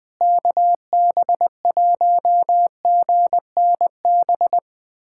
Morse code, heard as a keyed tone, about 20 wpm, 700 Hz. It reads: KB1GNB
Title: USA Callsign DE Callsign 20wpm